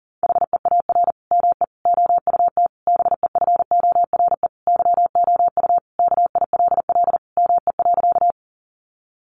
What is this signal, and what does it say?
Morse code, heard as a keyed tone, about 40 wpm, 700 Hz. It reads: HEAR ME OUT BEFORE YOU KILL ME.
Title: Sentences from Top 1400 Words 40wpm